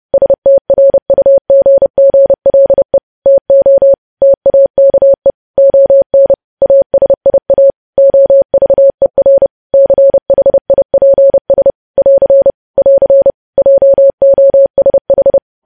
Morse code, heard as a keyed tone, about 30 wpm, 550 Hz. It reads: STRUGGLE TO TAKE ON ASIA OVER CHIPS <AR> <AR> JOSH
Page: news headlines in morse code at 30 wpm